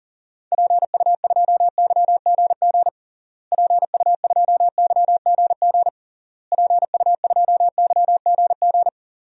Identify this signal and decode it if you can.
Morse code, heard as a keyed tone, about 40 wpm, 700 Hz. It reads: PU2YGG PU2YGG PU2YGG
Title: DXCC-100 Call Signs - 6 Characters - Repeated 3x - 40wpm